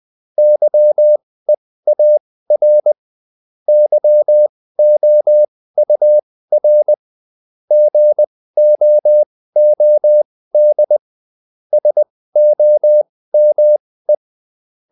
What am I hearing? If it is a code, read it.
Morse code, heard as a keyed tone, about 20 wpm, spaced as for 15 wpm, 600 Hz. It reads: YEAR YOUR GOOD SOME
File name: Top-100-Words-15